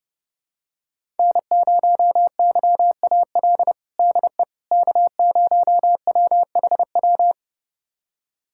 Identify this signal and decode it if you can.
Morse code, heard as a keyed tone, about 30 wpm, 700 Hz. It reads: N0YAL DE K0WHW
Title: USA Callsign DE Callsign - 5 Characters - 30wpm